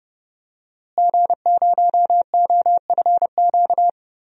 Morse code, heard as a keyed tone, about 32 wpm, 700 Hz. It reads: G0OFQ
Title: DXCC-100 Call Signs - 5 Characters - Repeated 2x - 30wpm